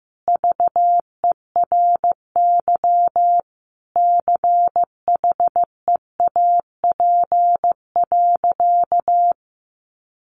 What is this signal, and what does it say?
Morse code, heard as a keyed tone, about 15 wpm, 700 Hz. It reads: VERY CHEAP.
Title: Sentences from Top 1200 Words 15wpm